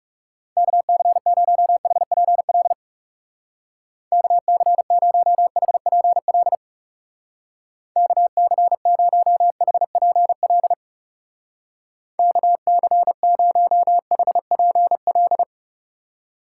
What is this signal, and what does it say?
Morse code, heard as a keyed tone, about 36 wpm, 700 Hz. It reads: KC0HPL KC0HPL KC0HPL KC0HPL
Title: Speed Racing - USA 2x3 Call Signs 30wpm